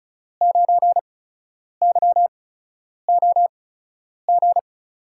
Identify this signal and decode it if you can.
Morse code, heard as a keyed tone, about 34 wpm, spaced as for 10 wpm, 700 Hz. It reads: 9YOG
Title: USA Call Signs - 6 Characters - 8x Character Spacing - 35wpm